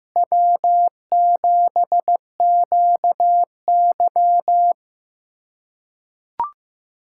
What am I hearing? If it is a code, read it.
Morse code, heard as a keyed tone, about 15 wpm, 700 Hz. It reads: W7QY
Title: USA Callsign DE Callsign - 4 Characters - 15wpm